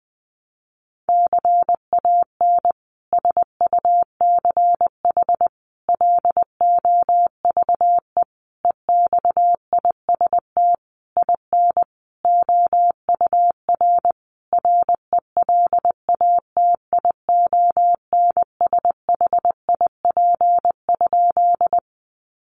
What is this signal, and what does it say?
Morse code, heard as a keyed tone, about 20 wpm, 700 Hz. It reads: CAN SUCH LOVE EXIST IN OUR RELATIONSHIP?
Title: Sentences from Top 1100 Words 20wpm